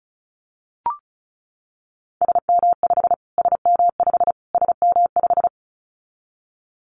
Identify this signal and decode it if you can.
Morse code, heard as a keyed tone, about 34 wpm, 700 Hz. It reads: SM5 SM5 SM5
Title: Callsign Prefixes for Top 25 DXCC - Repeated 3x - 35wpm